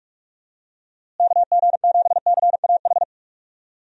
Morse code, heard as a keyed tone, about 46 wpm, 700 Hz. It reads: KG7CAH
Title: USA Call Signs - 6 Characters - Repeated 2x - 45wpm